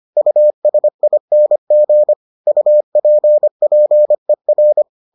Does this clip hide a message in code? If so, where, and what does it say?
Yes: Morse code from 0.2 to 4.8 s, heard as a keyed tone, about 25 wpm, 600 Hz. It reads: USING UPPER